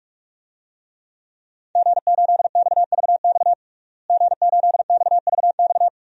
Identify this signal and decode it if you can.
Morse code, heard as a keyed tone, about 46 wpm, 700 Hz. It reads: G8XVX G8XVX
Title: DXCC-100 Call Signs - 5 Characters - Repeated 2x - 45wpm